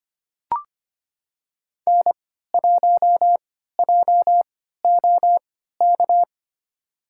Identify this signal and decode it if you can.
Morse code, heard as a keyed tone, about 25 wpm, spaced as for 14 wpm, 700 Hz. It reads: N1JOK
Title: USA Call Signs - 5 Characters - 3x Character Spacing - 25wpm